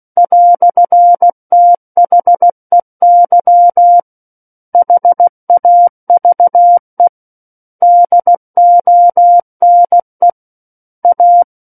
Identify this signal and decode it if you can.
Morse code, heard as a keyed tone, about 16 wpm, 700 Hz. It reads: "THEY HAVE DONE A
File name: Book One / 07-Chapter 7 - 16 WPM